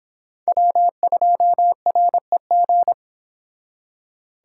Morse code, heard as a keyed tone, about 26 wpm, 700 Hz. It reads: W2REG